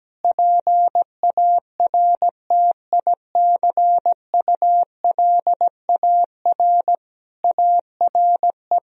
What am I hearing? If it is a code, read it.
Morse code, heard as a keyed tone, about 17 wpm, 700 Hz. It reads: PARTICULAR ARE
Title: Sentences from Top 1000 Words - Review Part 4 - 17wpm